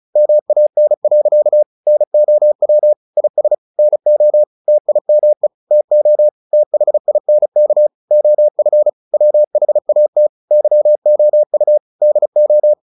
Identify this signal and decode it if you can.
Morse code, heard as a keyed tone, about 34 wpm, 600 Hz. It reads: MAN. NOW IS NO TIME TO THINK OF WHAT YOU DO
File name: the old man and the sea 2 35wpm